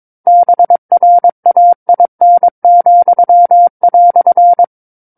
Morse code, heard as a keyed tone, about 22 wpm, 700 Hz. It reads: BRAIN,"
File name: Book One / 13-Chapter 13 - 22 WPM